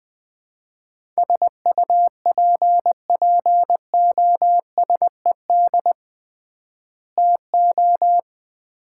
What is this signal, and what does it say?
Morse code, heard as a keyed tone, about 20 wpm, 700 Hz. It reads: SUPPOSED TO
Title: 3x Word-Spacing - Sets of 3 Words - 20wpm